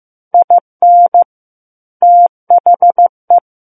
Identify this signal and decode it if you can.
Morse code, heard as a keyed tone, about 15 wpm, 700 Hz. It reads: IN THE
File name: Book One / 06-Chapter 6 - 15 WPM